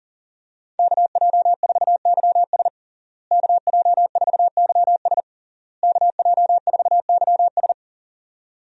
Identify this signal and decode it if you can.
Morse code, heard as a keyed tone, about 37 wpm, 700 Hz. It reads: KJ4YS KJ4YS KJ4YS
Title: USA Call Signs - 5 Characters - Repeated 3x - 40wpm